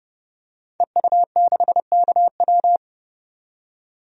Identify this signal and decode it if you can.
Morse code, heard as a keyed tone, about 30 wpm, 700 Hz. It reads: EU6KW
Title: DXCC-100 Call Signs - 5 Characters - 30wpm